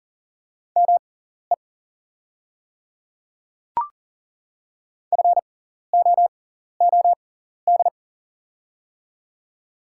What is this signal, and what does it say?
Morse code, heard as a keyed tone, about 40 wpm, spaced as for 14 wpm, 690 Hz. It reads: ME FOOD
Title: 6x Character Spacing - Top 500 Words - 40wpm